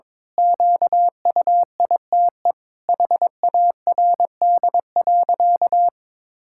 Morse code, heard as a keyed tone, about 22 wpm, 700 Hz. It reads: QUITE HARD.
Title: Pangram Challenge v2 - 22wpm